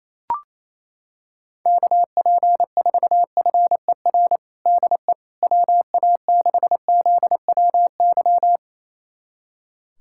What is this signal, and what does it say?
Morse code, heard as a keyed tone, about 28 wpm, 700 Hz. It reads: KP4FER DE WA6ZWY
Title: USA Callsign DE Callsign - 6 Characters - 28wpm